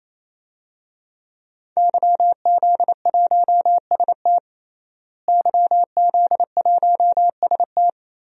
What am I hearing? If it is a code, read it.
Morse code, heard as a keyed tone, about 27 wpm, 700 Hz. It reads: YZ1ST YZ1ST
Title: DXCC-100 Call Signs - 5 Characters - Repeated 3x - 28wpm